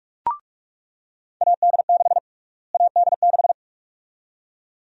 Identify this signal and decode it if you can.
Morse code, heard as a keyed tone, about 46 wpm, 700 Hz. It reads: AD6 AD6
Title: USA Call Sign Prefixes - Repeated 2x - 45wpm